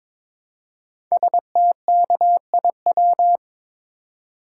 Morse code, heard as a keyed tone, about 22 wpm, 700 Hz. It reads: STKIW
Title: Code Groups - 5 Letters - 22wpm